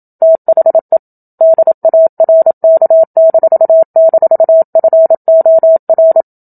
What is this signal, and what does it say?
Morse code, heard as a keyed tone, about 27 wpm, 650 Hz. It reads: THE DARK--FOR
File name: Book Two / 01-Chapter 1 - 27 WPM